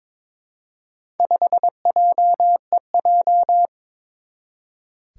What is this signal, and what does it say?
Morse code, heard as a keyed tone, about 22 wpm, 700 Hz. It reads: 5JEJ
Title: Four Challenging Letter-Number Combinations 22wpm